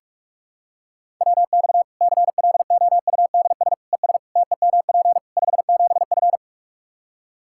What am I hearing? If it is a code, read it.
Morse code, heard as a keyed tone, about 42 wpm, 700 Hz. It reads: WX CLOUDS ES TEMP 57F
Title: Practice QSO - Exchange 2 - Info - 45wpm